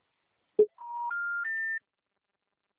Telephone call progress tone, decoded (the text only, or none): special information tone